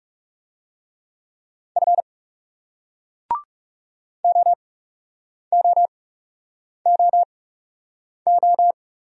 Morse code, heard as a keyed tone, about 42 wpm, spaced as for 8 wpm, 700 Hz. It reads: F OOOO